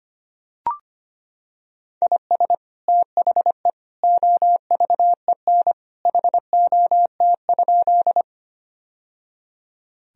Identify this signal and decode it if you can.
Morse code, heard as a keyed tone, about 25 wpm, 700 Hz. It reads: IS THE OVEN HOT?
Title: Sentences from Top 1000 Words - Review Part 4 - 25wpm